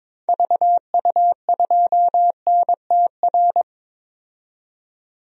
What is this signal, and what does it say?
Morse code, heard as a keyed tone, about 22 wpm, 700 Hz. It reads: VU2NTR
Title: DXCC-100 Call Signs - 6 Characters - 22wpm